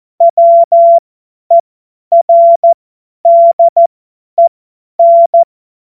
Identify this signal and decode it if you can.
Morse code, heard as a keyed tone, about 14 wpm, spaced as for 10 wpm, 680 Hz. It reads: WERDEN
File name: Witz-19-August-2025-50-F70